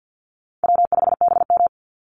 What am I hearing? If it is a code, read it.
Morse code, heard as a keyed tone, about 50 wpm, 700 Hz. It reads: W5BM
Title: Easy Call Signs - 50wpm